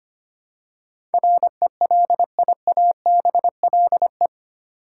Morse code, heard as a keyed tone, about 25 wpm, 700 Hz. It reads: RELIABLE
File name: 8 Letter Words - First 3 Letters Given - 25wpm